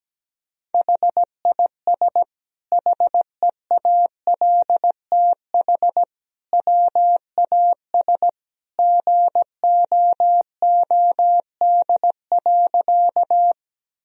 Morse code, heard as a keyed tone, about 17 wpm, 700 Hz. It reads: HIS HEALTH WAS GOOD.
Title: Sentences from Top 1000 Words - Review Part 3 - 17wpm